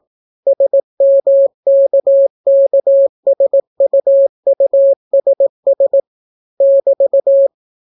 Morse code, heard as a keyed tone, about 18 wpm, 550 Hz. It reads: SMKKSUUSS <BT>